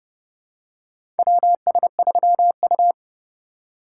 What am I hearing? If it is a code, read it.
Morse code, heard as a keyed tone, about 30 wpm, 700 Hz. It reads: WS3U
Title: USA Call Signs - 4 Characters - 4x Character Spacing - 30wpm